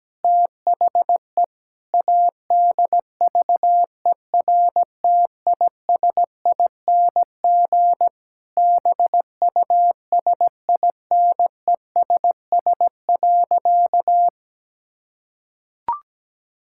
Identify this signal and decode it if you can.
Morse code, heard as a keyed tone, about 17 wpm, 700 Hz. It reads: THE ADVERTISING BUSINESS.
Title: Sentences from Top 1000 Words - Review Part 6 - 17wpm